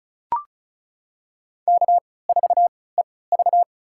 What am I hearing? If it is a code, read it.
Morse code, heard as a keyed tone, about 36 wpm, spaced as for 20 wpm, 700 Hz. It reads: K4EV
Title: USA Call Signs - 4 Characters - 3x Character Spacing - 35wpm